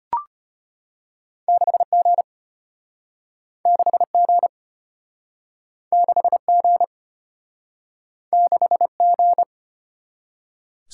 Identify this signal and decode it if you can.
Morse code, heard as a keyed tone, about 33 wpm, 700 Hz. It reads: 6G 6G 6G 6G